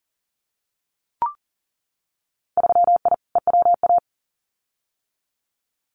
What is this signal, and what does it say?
Morse code, heard as a keyed tone, about 40 wpm, 700 Hz. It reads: 3I EWA
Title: Winter Field Day Exchanges Encore 40wpm